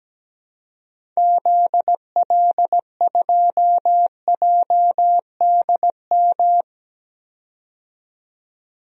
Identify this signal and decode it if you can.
Morse code, heard as a keyed tone, about 17 wpm, 700 Hz. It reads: ZL2JDM
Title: DXCC-100 Call Signs - 6 Characters - 17wpm